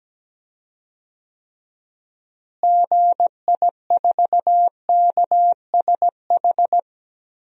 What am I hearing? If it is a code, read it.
Morse code, heard as a keyed tone, about 17 wpm, 700 Hz. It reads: GI4KSH
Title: DXCC-100 Call Signs - 6 Characters - Repeated 2x - 17wpm